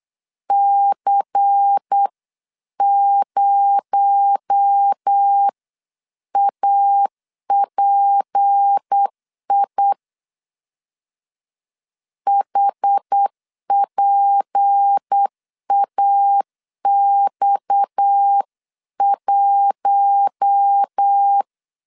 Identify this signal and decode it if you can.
Morse code, heard as a keyed tone, about 8 wpm, 800 Hz. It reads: C0API HPAX1
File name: LEZIONI / 8 YJ45 / 8 YJ4 8wpm 3246